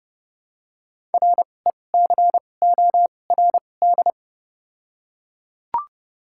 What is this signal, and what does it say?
Morse code, heard as a keyed tone, about 31 wpm, spaced as for 22 wpm, 700 Hz. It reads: RECORD
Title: 2x Character Spacing - Top 500 Words - 30wpm